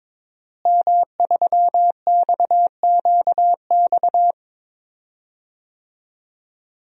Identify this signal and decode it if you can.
Morse code, heard as a keyed tone, about 22 wpm, 700 Hz. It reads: M3XQX
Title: DXCC-100 Call Signs - 5 Characters - Repeated 3x - 22wpm